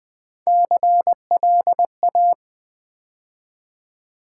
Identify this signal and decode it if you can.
Morse code, heard as a keyed tone, about 20 wpm, 700 Hz. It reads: CLA
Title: Build A Word - 3 Letters At A Time - 8-Letter Words - 20wpm